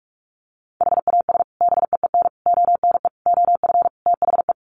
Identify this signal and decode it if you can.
Morse code, heard as a keyed tone, about 44 wpm, 700 Hz. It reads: HAS BEEN ONE OF THE
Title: Sets of 5 Words Encore 45wpm